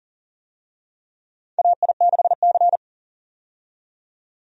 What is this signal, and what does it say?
Morse code, heard as a keyed tone, about 40 wpm, 700 Hz. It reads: AI6C
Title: USA Call Signs - 4 Characters - 40wpm